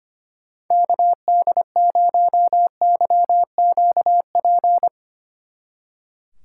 Morse code, heard as a keyed tone, about 31 wpm, 700 Hz. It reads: KD0YQP
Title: Speed Racing - USA 2x3 Call Signs 25wpm